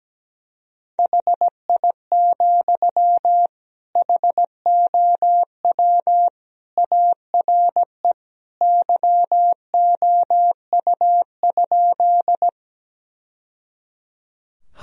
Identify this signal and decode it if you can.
Morse code, heard as a keyed tone, about 17 wpm, 700 Hz. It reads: HI, HOW ARE YOU?
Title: Sentences from Top 1000 Words - Review Part 3 - 17wpm